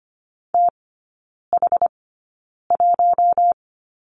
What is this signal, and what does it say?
Morse code, heard as a keyed tone, about 25 wpm, spaced as for 9 wpm, 700 Hz. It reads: TH1
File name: Lesson 15e - Number 5 - Three Characters 25wpm